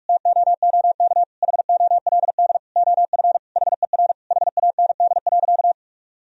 Morse code, heard as a keyed tone, about 41 wpm, 700 Hz. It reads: TOOK HOLD OF HER HAND.
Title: Sentences from Top 1000 Words - Review Part 2 - 45wpm